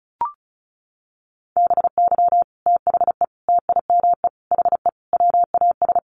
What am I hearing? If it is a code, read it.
Morse code, heard as a keyed tone, about 34 wpm, 700 Hz. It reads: BY THE TIME HE WAS
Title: Sets of 5 Words Encore 35wpm